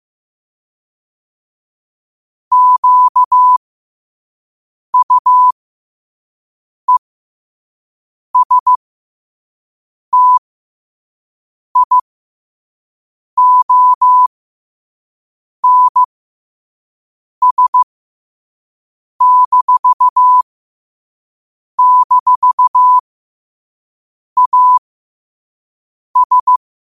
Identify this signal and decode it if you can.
Morse code, heard as a keyed tone, about 15 wpm, spaced as for 5 wpm, 1000 Hz. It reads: QUESTIONS--AS
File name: qotdfiles / 5wpm